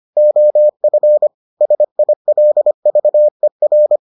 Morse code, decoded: OF SILVER

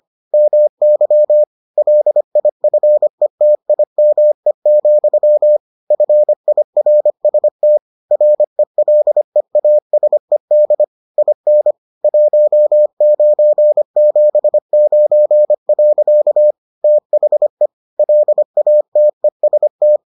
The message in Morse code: MY LIFETIME, FIRST RELEASED IN 1979. THE LATEST

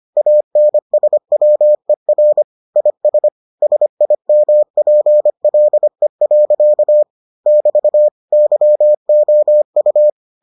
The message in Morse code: ANSWER IS SIMPLE. <BT> YOU

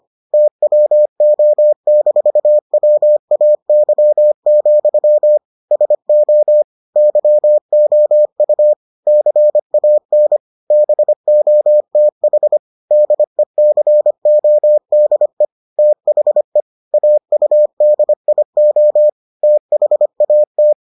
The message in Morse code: TWO-WAY, SO YOU CAN BOTH DECODE THE AUDIO THAT

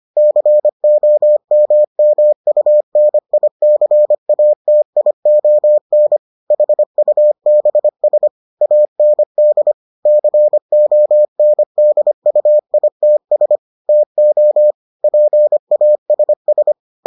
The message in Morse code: COMMUNICATION HUBS AND CONDUITS TO PASS

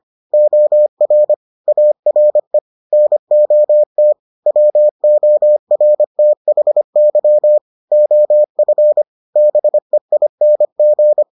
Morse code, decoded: OR ARE NOT WORTHY OF BEING

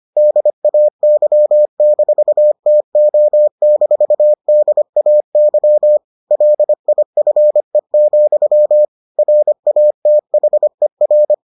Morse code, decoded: DAY-TO-DAY LIFE, RATHER

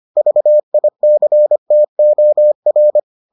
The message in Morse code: VICTOR

